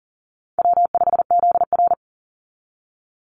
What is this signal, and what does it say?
Morse code, heard as a keyed tone, about 40 wpm, 700 Hz. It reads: W5ZR
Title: Easy Call Signs Repeated 2x - 40wpm